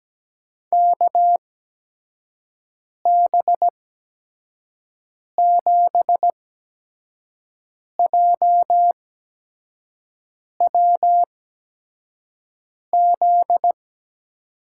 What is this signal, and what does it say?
Morse code, heard as a keyed tone, about 17 wpm, spaced as for 5 wpm, 700 Hz. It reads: KB7JWZ
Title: USA Call Signs - 6 Characters - 8x Character Spacing - 17wpm